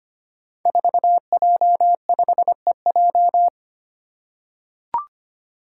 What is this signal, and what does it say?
Morse code, heard as a keyed tone, about 25 wpm, 700 Hz. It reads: 4J5EJ
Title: Five Challenging Letter-Number Combinations 25wpm